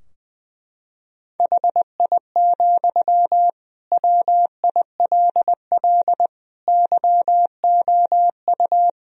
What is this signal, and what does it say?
Morse code, heard as a keyed tone, about 20 wpm, 700 Hz. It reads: HI, WILL YOU
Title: Sentences from Top 1000 Words - Review Part 3 - 20wpm